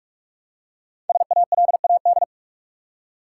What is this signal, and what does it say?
Morse code, heard as a keyed tone, about 46 wpm, 700 Hz. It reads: SALAD
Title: Five Letter Words - 45wpm